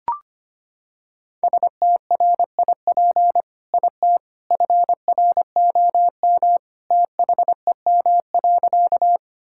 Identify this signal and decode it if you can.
Morse code, heard as a keyed tone, about 25 wpm, 700 Hz. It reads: STRIP IT FROM THEM.
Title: Sentences from Top 1600 Words 25wpm